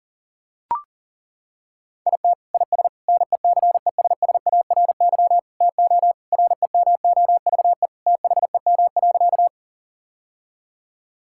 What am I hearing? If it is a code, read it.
Morse code, heard as a keyed tone, about 40 wpm, 700 Hz. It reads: IT IS NECESSARY TO REMOVE THEM.
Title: Sentences from Top 1000 Words - Review Part 5 - 40wpm